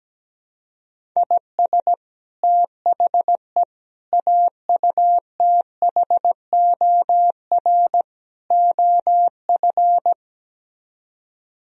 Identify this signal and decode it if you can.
Morse code, heard as a keyed tone, about 17 wpm, 700 Hz. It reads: IS THE AUTHOR OF
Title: Speed-Racing - Sets of 4 Words 17wpm